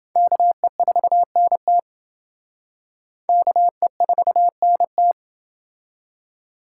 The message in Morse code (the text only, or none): KE4NT KE4NT